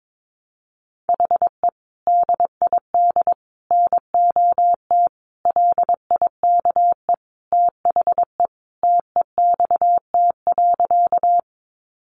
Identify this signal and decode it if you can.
Morse code, heard as a keyed tone, about 22 wpm, 700 Hz. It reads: HE DID NOT LIKE THE TEXT.
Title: Sentences from Top 1200 Words 22wpm